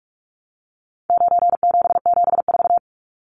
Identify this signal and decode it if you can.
Morse code, heard as a keyed tone, about 44 wpm, 700 Hz. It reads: 9774
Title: Numbers - 4-Digit 45wpm